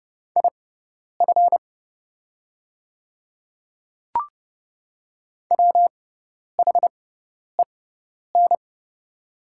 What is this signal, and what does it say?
Morse code, heard as a keyed tone, about 30 wpm, spaced as for 10 wpm, 690 Hz. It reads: IF WHEN
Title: 6x Character Spacing - Top 100 Words - 30wpm